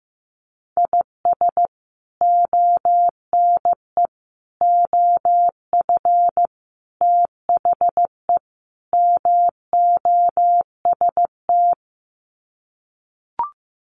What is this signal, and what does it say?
Morse code, heard as a keyed tone, about 15 wpm, 700 Hz. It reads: IS ONE OF THE MOST
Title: Sets of 5 Words 15wpm